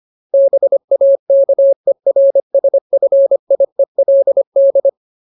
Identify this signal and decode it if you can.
Morse code, heard as a keyed tone, about 25 wpm, 550 Hz. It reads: BAKERSFIELD
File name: Ultimate Word List / 25WPM Ultimate